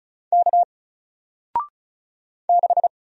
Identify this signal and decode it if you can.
Morse code, heard as a keyed tone, about 36 wpm, 700 Hz. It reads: K 6
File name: Single-Letter-Number-35wpm